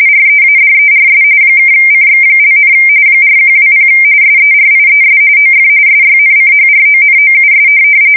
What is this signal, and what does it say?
RTTY, about 45.45 baud. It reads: READERS (CX), 4120 SERIES OF SYNCHRONOUS
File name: fsk170 / teletype-corp